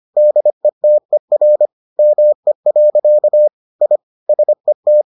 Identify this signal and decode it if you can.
Morse code, heard as a keyed tone, about 25 wpm, 600 Hz. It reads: DETER ME. I SET